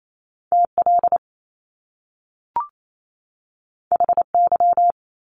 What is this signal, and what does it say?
Morse code, heard as a keyed tone, about 28 wpm, 700 Hz. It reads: TL HY
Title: Top Two Letter Combinations in Words 28wpm